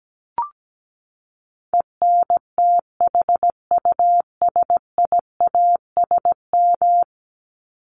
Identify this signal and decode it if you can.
Morse code, heard as a keyed tone, about 17 wpm, 700 Hz. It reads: ENTHUSIASM
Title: Top 1751-2000 Words 17wpm